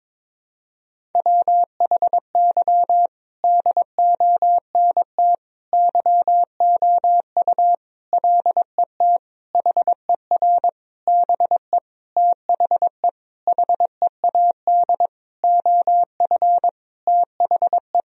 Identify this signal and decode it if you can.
Morse code, heard as a keyed tone, about 22 wpm, 700 Hz. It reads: WHY DONT YOU LET HER BE THE HEAD OF THE
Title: Sentences from Top 1000 Words - Review Part 1 - 22wpm